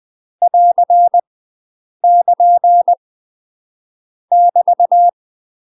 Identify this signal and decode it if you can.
Morse code, heard as a keyed tone, about 20 wpm, 700 Hz. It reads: <AR> <KN> <BT>
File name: UK MASTER QSOs TRAINING / 20WPM / UK MASTER QSOs 0002